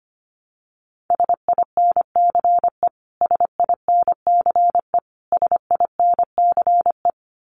Morse code, heard as a keyed tone, about 25 wpm, 700 Hz. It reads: SINCE SINCE SINCE
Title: Top 500 Words Repeated 25wpm